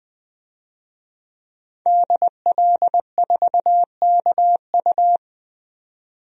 Morse code, heard as a keyed tone, about 20 wpm, 700 Hz. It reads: DL4KU
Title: DXCC-100 Call Signs - 5 Characters - 20wpm